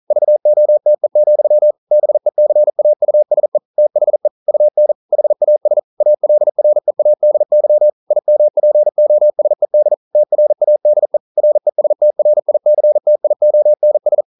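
Morse code, heard as a keyed tone, about 41 wpm, 600 Hz. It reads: VOTE, BECAUSE THE UN HAS ALREADY IMPOSED TRADE RESTRICTIONS
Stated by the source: Morse code practice at 41 wpm, from CW med Gnister.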